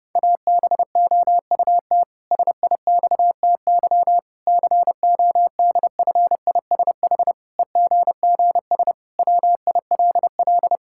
Morse code, heard as a keyed tone, about 30 wpm, 700 Hz. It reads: ABOUT SIXTY CODFISH EGGS WILL
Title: Pangram Challenge v2 - 30wpm